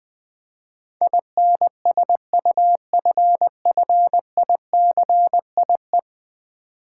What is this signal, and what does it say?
Morse code, heard as a keyed tone, about 20 wpm, 700 Hz. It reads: INSUFFICIE
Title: Build A Word - 5 Letters At A Time - 12-Letter Words - 20wpm